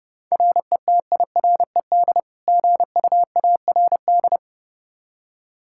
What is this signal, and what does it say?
Morse code, heard as a keyed tone, about 30 wpm, 700 Hz. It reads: RETIRED GUARD
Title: Practice QSO - Exchange 3 - More Info - 30wpm